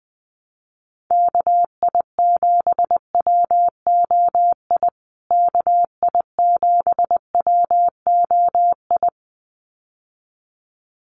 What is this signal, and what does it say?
Morse code, heard as a keyed tone, about 20 wpm, 700 Hz. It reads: KI7WOI KI7WOI
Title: USA Call Signs Repeated 2x 20wpm